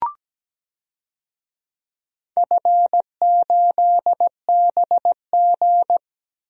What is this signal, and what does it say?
Morse code, heard as a keyed tone, about 17 wpm, 700 Hz. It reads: F8BG
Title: DXCC-100 Call Signs - 4 Characters - Repeated 2x - 17wpm